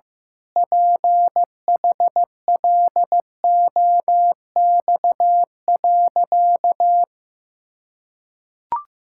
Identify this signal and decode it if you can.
Morse code, heard as a keyed tone, about 15 wpm, 700 Hz. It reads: PHLOX.
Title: Pangram Challenge v2 - 15wpm